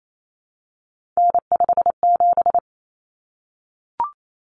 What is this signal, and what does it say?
Morse code, heard as a keyed tone, about 28 wpm, 700 Hz. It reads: N57
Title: Common Cut Numbers - 3-Digit 28wpm